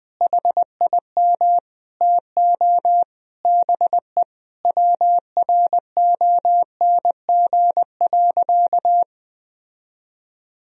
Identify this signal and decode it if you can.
Morse code, heard as a keyed tone, about 20 wpm, 700 Hz. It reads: HIM TO BE WRONG.
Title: Sentences from Top 1000 Words - Review Part 3 - 20wpm